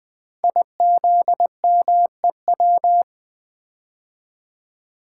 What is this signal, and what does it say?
Morse code, heard as a keyed tone, about 20 wpm, 700 Hz. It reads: IZMEW
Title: Code Groups - 5 Letters - 20wpm